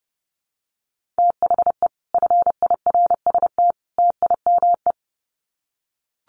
Morse code, heard as a keyed tone, about 30 wpm, 700 Hz. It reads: THE FIRST TIME
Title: Sets of 3 Words 30wpm